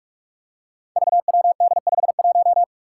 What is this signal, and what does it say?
Morse code, heard as a keyed tone, about 46 wpm, 700 Hz. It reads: VWD51
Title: Five Challenging Letter-Number Combinations 45wpm